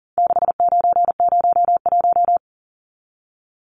Morse code, heard as a keyed tone, about 40 wpm, 700 Hz. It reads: 6901
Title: Numbers - 4-Digit 40wpm